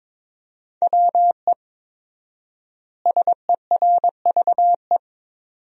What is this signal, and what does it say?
Morse code, heard as a keyed tone, about 21 wpm, 700 Hz. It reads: WE SERVE
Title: 4x Word-Spacing - Sentences from Top 400 Words - 22wpm